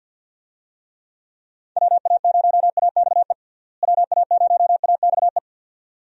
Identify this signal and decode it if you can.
Morse code, heard as a keyed tone, about 46 wpm, 700 Hz. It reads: WA0AXE WA0AXE
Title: USA Call Signs - 6 Characters - Repeated 2x - 50wpm